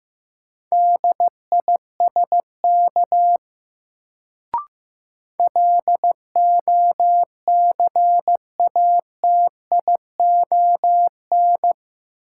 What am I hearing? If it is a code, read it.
Morse code, heard as a keyed tone, about 16 wpm, 700 Hz. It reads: DISK LOCATION
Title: Sentences from Top 1000 Words - Review Part 6 - New Word Drill - 15wpm